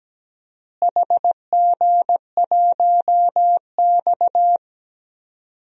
Morse code, heard as a keyed tone, about 17 wpm, 700 Hz. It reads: HG1X
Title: DXCC-100 Call Signs - 4 Characters - Repeated 2x - 17wpm